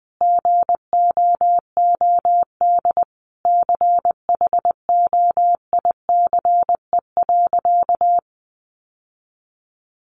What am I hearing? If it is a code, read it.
Morse code, heard as a keyed tone, about 20 wpm, 700 Hz. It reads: GOOD CHOICE.
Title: Sentences from Top 900 Words 20wpm